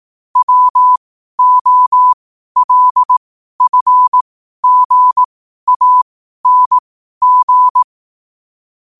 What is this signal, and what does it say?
Morse code, heard as a keyed tone, about 18 wpm, spaced as for 12 wpm, 1000 Hz. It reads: WOLFGANG
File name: qotdfiles / 12wpm